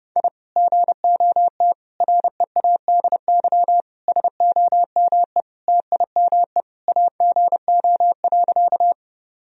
Morse code, heard as a keyed tone, about 30 wpm, 700 Hz. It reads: I GOT READY SOME TIME AGO.
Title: Sentences from Top 1000 Words - Review Part 3 - 30wpm